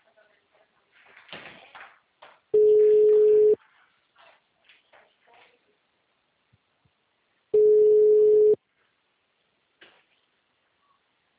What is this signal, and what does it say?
Ringback tone (Europe, 425 Hz), with speech.
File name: out- -2034- - - .0